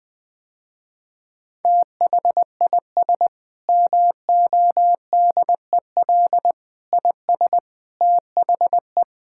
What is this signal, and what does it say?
Morse code, heard as a keyed tone, about 19 wpm, 700 Hz. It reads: THIS MODEL IS THE
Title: Sentences with 7-Word Phrases - Spoken Beginning - 20wpm